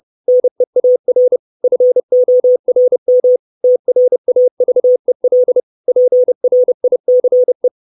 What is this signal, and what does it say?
Morse code, heard as a keyed tone, about 30 wpm, 500 Hz. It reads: NEAR FORM TRAVEL PRICE
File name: 632 Words 30WPM